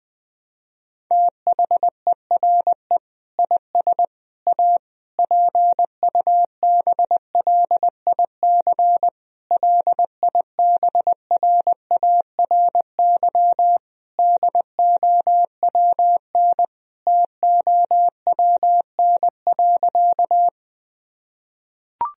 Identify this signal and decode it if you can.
Morse code, heard as a keyed tone, about 20 wpm, 700 Hz. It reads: THERE IS A PUBLIC LIBRARY DOWNTOWN.
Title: Sentences from Top 1000 Words - Review Part 4 - 20wpm